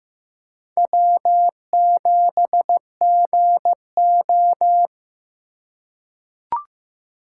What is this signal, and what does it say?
Morse code, heard as a keyed tone, about 15 wpm, 700 Hz. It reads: W7GO
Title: USA Call Signs - 3x Character Spacing - 15wpm